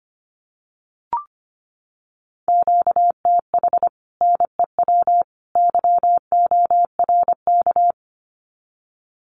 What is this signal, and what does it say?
Morse code, heard as a keyed tone, about 25 wpm, 700 Hz. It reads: QTH NEW YORK
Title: Common QSO Phrases 25wpm